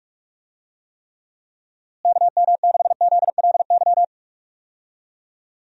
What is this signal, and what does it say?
Morse code, heard as a keyed tone, about 46 wpm, 700 Hz. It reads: KM6ZLY
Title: USA Call Signs - 6 Characters - Repeated 2x - 45wpm